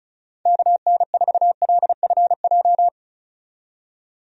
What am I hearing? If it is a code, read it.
Morse code, heard as a keyed tone, about 34 wpm, 700 Hz. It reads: KN4LFJ
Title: USA Call Signs - 6 Characters - 35wpm